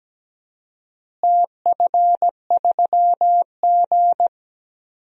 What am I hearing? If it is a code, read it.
Morse code, heard as a keyed tone, about 17 wpm, 700 Hz. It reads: TF3G
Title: DXCC-100 Call Signs - 4 Characters - Repeated 2x - 17wpm